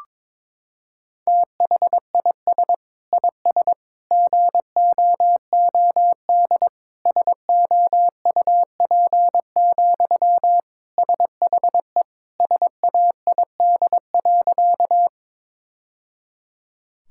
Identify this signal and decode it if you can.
Morse code, heard as a keyed tone, about 21 wpm, 700 Hz. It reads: THIS IS GOOD SOUP, SHE SAID.
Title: Sentences from Top 1000 Words - Review Part 6 - 22wpm